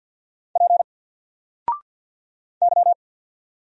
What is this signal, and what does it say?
Morse code, heard as a keyed tone, about 48 wpm, 700 Hz. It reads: P Y
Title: Single Character (Letter:Number:Punctuation:Prosign) 50wpm